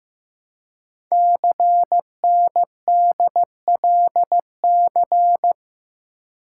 Morse code, heard as a keyed tone, about 15 wpm, 700 Hz. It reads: CNDLC
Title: Code Groups - 5 Letters - 15wpm